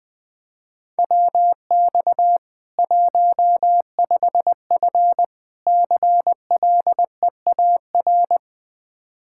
Morse code, heard as a keyed tone, about 20 wpm, 700 Hz. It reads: WX 15F CLEAR
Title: Practice QSO - Exchange 2 - Info - 20wpm